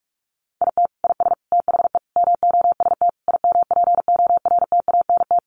Morse code, heard as a keyed tone, about 44 wpm, 700 Hz. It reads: IT IS THE MOST IMPORTANT
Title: Sentences from Top 1500 Words 45wpm